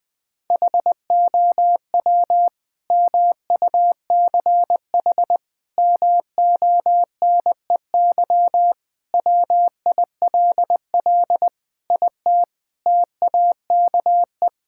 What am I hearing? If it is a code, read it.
Morse code, heard as a keyed tone, about 20 wpm, 700 Hz. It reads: HOW MUCH MONEY WILL IT TAKE
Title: Sentences from Top 1000 Words - Review Part 5 - 20wpm